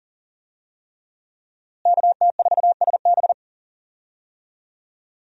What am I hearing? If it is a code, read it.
Morse code, heard as a keyed tone, about 39 wpm, 700 Hz. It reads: KT4SB
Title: USA Call Signs - 5 Characters - Repeated 3x - 40wpm